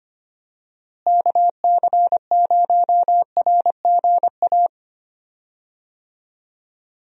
Morse code, heard as a keyed tone, about 25 wpm, 700 Hz. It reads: KC0RGA
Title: USA Call Signs - 6 Characters - Repeated 2x - 25wpm